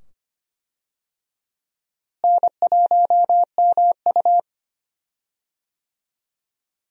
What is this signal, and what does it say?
Morse code, heard as a keyed tone, about 25 wpm, 700 Hz. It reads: N1MU
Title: USA Call Signs - 4 Characters - Repeated 2x - 25wpm